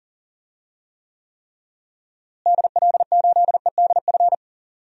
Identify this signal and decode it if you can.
Morse code, heard as a keyed tone, about 40 wpm, 700 Hz. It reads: DL8EDF
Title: DXCC-100 Call Signs - 6 Characters - Repeated 3x - 40wpm